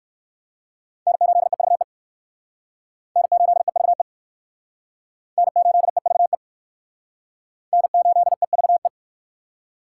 Morse code, heard as a keyed tone, about 57 wpm, 700 Hz. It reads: N8EVE N8EVE N8EVE N8EVE